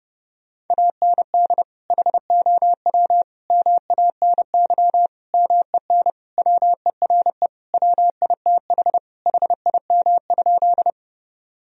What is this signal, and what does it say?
Morse code, heard as a keyed tone, about 29 wpm, 700 Hz. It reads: AND HOW MANY MEN WERE WITH HIM?
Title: Sentences from Top 1000 Words - Review Part 1 - 30wpm